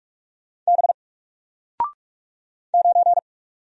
Morse code, heard as a keyed tone, about 45 wpm, 700 Hz. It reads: B 9
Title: Single Letter-Number Alt 45wpm